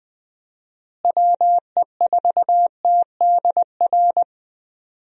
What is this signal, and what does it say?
Morse code, heard as a keyed tone, about 20 wpm, 700 Hz. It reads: WE4TDR
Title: USA Call Signs - 6 Characters - Repeated 3x - 20wpm